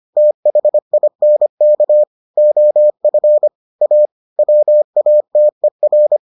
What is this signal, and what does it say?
Morse code, heard as a keyed tone, about 25 wpm, 600 Hz. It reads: THINK OF A WATER